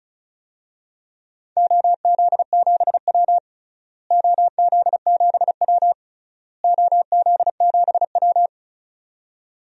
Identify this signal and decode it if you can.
Morse code, heard as a keyed tone, about 34 wpm, 700 Hz. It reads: OZ7W OZ7W OZ7W
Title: DXCC-100 Call Signs - 4 Characters - Repeated 3x - 35wpm